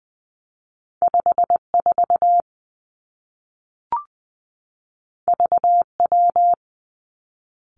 Morse code, heard as a keyed tone, about 20 wpm, 700 Hz. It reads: 5 4 V W
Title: Two Challenging Letter-Number Combinations 20wpm